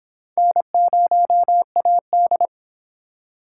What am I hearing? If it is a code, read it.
Morse code, heard as a keyed tone, about 26 wpm, 700 Hz. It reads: N0AD